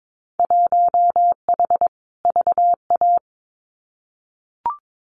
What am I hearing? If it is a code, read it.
Morse code, heard as a keyed tone, about 22 wpm, 700 Hz. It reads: 1H VA
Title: Winter Field Day Exchanges 22wpm